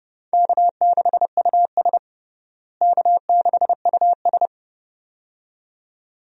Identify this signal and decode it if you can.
Morse code, heard as a keyed tone, about 32 wpm, 700 Hz. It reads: K6US K6US
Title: USA Call Signs - 4 Characters - Repeated 2x - 30wpm